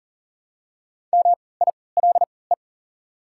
Morse code, heard as a keyed tone, about 40 wpm, spaced as for 23 wpm, 690 Hz. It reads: MILE
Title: 3x Character Spacing - Top 500 Words - 40wpm